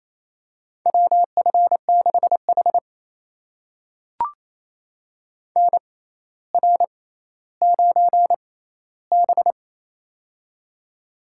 Morse code, heard as a keyed tone, about 28 wpm, spaced as for 14 wpm, 700 Hz. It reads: WF6H NR9B
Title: USA Call Signs - 4 Characters - 6x Character Spacing - 28wpm